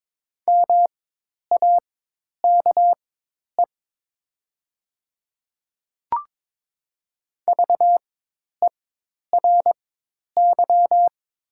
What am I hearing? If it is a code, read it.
Morse code, heard as a keyed tone, about 22 wpm, spaced as for 10 wpm, 700 Hz. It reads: MAKE VERY
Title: 4x Character Spacing - Top 300 Words - 22wpm